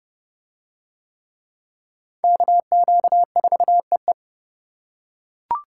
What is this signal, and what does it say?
Morse code, heard as a keyed tone, about 27 wpm, 700 Hz. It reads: KQ4EE